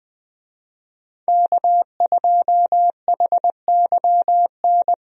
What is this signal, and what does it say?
Morse code, heard as a keyed tone, about 21 wpm, 700 Hz. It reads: K2HYN
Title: USA Call Signs - 5 Characters - 20wpm